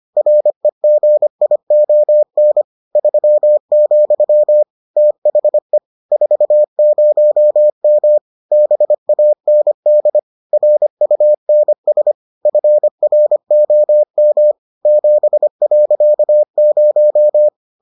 Morse code, heard as a keyed tone, about 25 wpm, 600 Hz. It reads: REGION 3, THE 40M BAND RUNS FROM 7.0